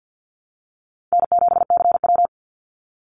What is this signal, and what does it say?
Morse code, heard as a keyed tone, about 50 wpm, 700 Hz. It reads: N7CW
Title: NA SOTA Activator Callsigns 50wpm